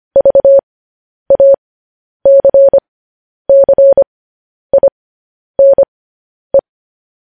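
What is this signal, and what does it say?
Morse code, heard as a keyed tone, about 25 wpm, spaced as for 10 wpm, 550 Hz. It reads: VACCINE
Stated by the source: Morse code news headlines broadcast at 10 wpm.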